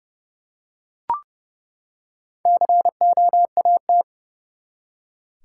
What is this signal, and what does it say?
Morse code, heard as a keyed tone, about 30 wpm, 700 Hz. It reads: COAT
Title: Four Letter Words - 30wpm